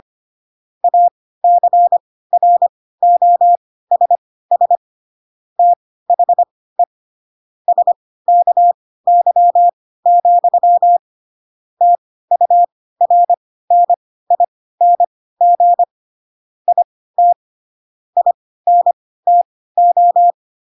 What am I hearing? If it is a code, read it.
Morse code, heard as a keyed tone, about 25 wpm, spaced as for 16 wpm, 700 Hz. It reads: ACROSS THE SKY, TURNING IT INTO